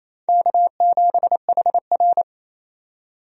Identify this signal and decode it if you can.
Morse code, heard as a keyed tone, about 28 wpm, 700 Hz. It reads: K7HR
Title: USA Call Signs - 4 Characters - 28wpm